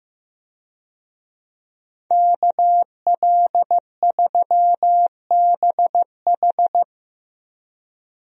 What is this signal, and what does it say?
Morse code, heard as a keyed tone, about 15 wpm, 700 Hz. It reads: KL3BH
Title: USA Call Signs - 5 Characters - Repeated 3x - 15wpm